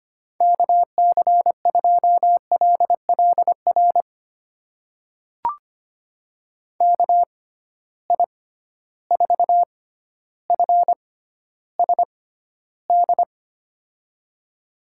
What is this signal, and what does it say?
Morse code, heard as a keyed tone, about 25 wpm, spaced as for 13 wpm, 700 Hz. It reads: KC2LLR KI4FSD
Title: USA Call Signs - 6x Character Spacing - 25wpm